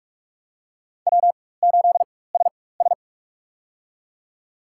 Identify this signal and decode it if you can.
Morse code, heard as a keyed tone, about 45 wpm, spaced as for 21 wpm, 700 Hz. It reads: W8SS
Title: USA Call Signs - 4 Characters - 4x Character Spacing - 45wpm